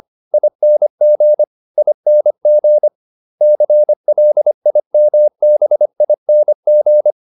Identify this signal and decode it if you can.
Morse code, heard as a keyed tone, about 25 wpm, 600 Hz. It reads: ING ING CLIMBING